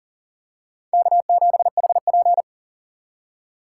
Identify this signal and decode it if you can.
Morse code, heard as a keyed tone, about 40 wpm, 700 Hz. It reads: K7HP
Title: USA Call Signs - 4 Characters - 6x Character Spacing - 40wpm